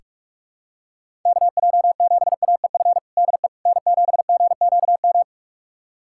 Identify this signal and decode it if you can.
Morse code, heard as a keyed tone, about 45 wpm, 700 Hz. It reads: KJ7AEF DE N7GQM
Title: USA Callsign DE Callsign 45wpm